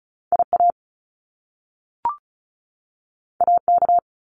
Morse code, heard as a keyed tone, about 36 wpm, 700 Hz. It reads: IA AK
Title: US States 35wpm